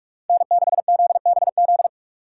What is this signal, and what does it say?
Morse code, heard as a keyed tone, about 44 wpm, 700 Hz. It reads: N6ZBZ